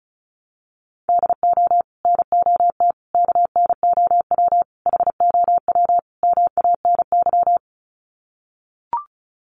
Morse code, heard as a keyed tone, about 34 wpm, 700 Hz. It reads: DO NOT KNOW HOW MANY
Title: Sets of 5 Words Encore 35wpm